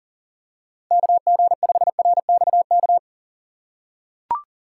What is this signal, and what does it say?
Morse code, heard as a keyed tone, about 40 wpm, 700 Hz. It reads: KG5RXK
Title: USA Call Signs - 6 Characters - 3x Character Spacing - 40wpm